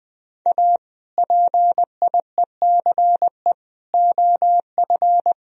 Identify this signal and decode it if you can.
Morse code, heard as a keyed tone, about 20 wpm, 700 Hz. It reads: A PIECE OF
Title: Speed-Racing - Sets of 3 Words 17wpm